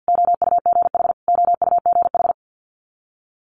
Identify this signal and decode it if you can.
Morse code, heard as a keyed tone, about 50 wpm, 700 Hz. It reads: OUGH OUGH
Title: Multi-letter Phonemes Repeated 3x 50wpm